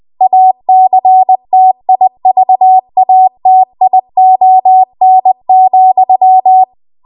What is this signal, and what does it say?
Morse code, heard as a keyed tone, about 20 wpm, 750 Hz. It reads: ACTIVATION,
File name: Archive / 20 WPM / 20WPM